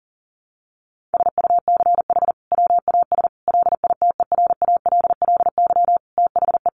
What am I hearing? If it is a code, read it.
Morse code, heard as a keyed tone, about 40 wpm, 700 Hz. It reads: SUCH WAS LITERALLY THE
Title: Sentences from Top 1400 Words 40wpm